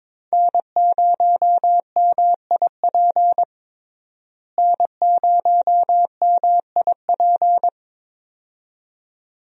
Morse code, heard as a keyed tone, about 22 wpm, 700 Hz. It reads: N0MIP N0MIP
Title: USA Call Signs - 5 Characters - Repeated 2x - 22wpm